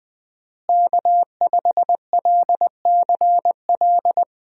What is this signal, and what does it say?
Morse code, heard as a keyed tone, about 20 wpm, 700 Hz. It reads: K5LCL
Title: USA Call Signs - 5 Characters - Repeated 3x - 20wpm